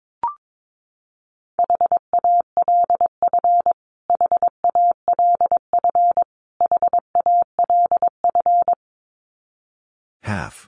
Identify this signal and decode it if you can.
Morse code, heard as a keyed tone, about 22 wpm, 700 Hz. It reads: HALF HALF HALF
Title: Top 500 Words Repeated 22wpm